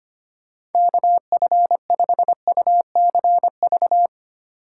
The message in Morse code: KF5UCV

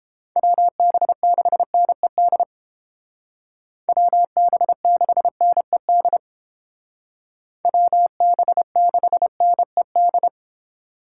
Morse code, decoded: WB6NED WB6NED WB6NED

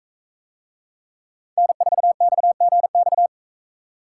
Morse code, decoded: N4XGX

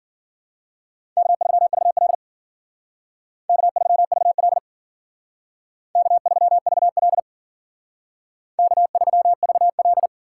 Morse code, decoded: K3VL K3VL K3VL K3VL

